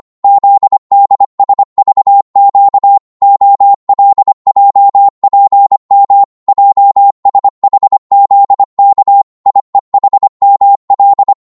Morse code, heard as a keyed tone, about 25 wpm, 800 Hz. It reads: ZDSVQ OLJPM JSHZK IEHML
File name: Morse / ltr-25wpm